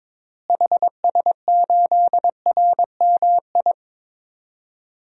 Morse code, heard as a keyed tone, about 22 wpm, 700 Hz. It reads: HS8RMI
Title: DXCC-100 Call Signs - 6 Characters - Repeated 3x - 22wpm